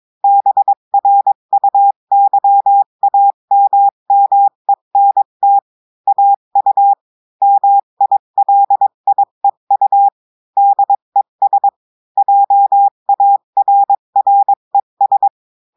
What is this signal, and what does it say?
Morse code, heard as a keyed tone, about 22 wpm, 800 Hz. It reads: BRUYAMMENT AU MILIEU DES JARRES
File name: cours-du-mercredi / audios / cours-853-du-2-mai-2024-etxte chiffre et texte clair-a-20-wpm